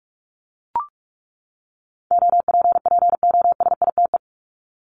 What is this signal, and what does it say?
Morse code, heard as a keyed tone, about 44 wpm, 700 Hz. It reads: OPPOSITE
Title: Top 1251-1500 Words 45wpm